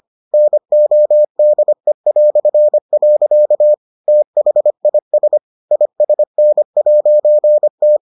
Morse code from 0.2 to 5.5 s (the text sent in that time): NODE". THIS